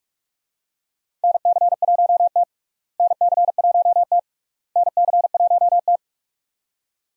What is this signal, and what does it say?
Morse code, heard as a keyed tone, about 45 wpm, 700 Hz. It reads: NC1T NC1T NC1T
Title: USA Call Signs - 4 Characters - Repeated 3x - 45wpm